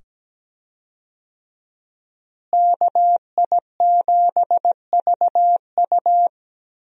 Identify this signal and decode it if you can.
Morse code, heard as a keyed tone, about 17 wpm, 700 Hz. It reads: KI7VU
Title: USA Call Signs - 5 Characters - Repeated 2x - 17wpm